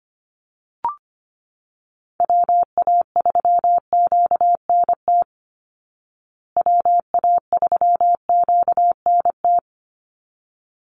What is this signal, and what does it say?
Morse code, heard as a keyed tone, about 25 wpm, 700 Hz. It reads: WA3QNT WA3QNT
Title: Call Signs Repeated 3x - 25wpm